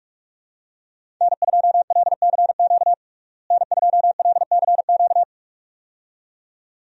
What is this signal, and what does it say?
Morse code, heard as a keyed tone, about 42 wpm, 700 Hz. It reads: N2LCQ N2LCQ
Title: USA Call Signs - 5 Characters - Repeated 2x - 45wpm